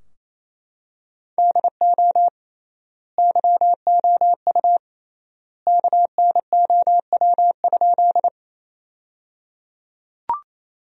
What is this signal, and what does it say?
Morse code, heard as a keyed tone, about 28 wpm, 700 Hz. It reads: DO YOU KNOW?
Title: 3x Word-Spacing - Sentences from Top 100 Words - 28wpm